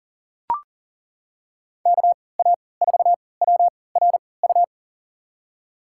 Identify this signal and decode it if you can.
Morse code, heard as a keyed tone, about 40 wpm, spaced as for 23 wpm, 700 Hz. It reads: KA4WRU
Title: USA Call Signs - 6 Characters - 3x Character Spacing - 40wpm